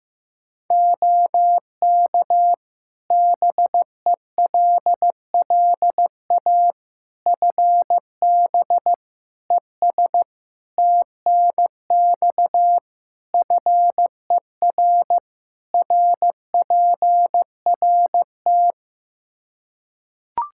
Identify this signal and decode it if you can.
Morse code, heard as a keyed tone, about 15 wpm, 700 Hz. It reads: OK BELLA FB ES TNX FER RPRT
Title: K1USN SST QSO Practice 15wpm